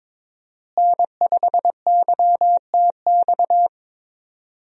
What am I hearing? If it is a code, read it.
Morse code, heard as a keyed tone, about 22 wpm, 700 Hz. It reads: N5YTX
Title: USA Call Signs - 5 Characters - Repeated 3x - 22wpm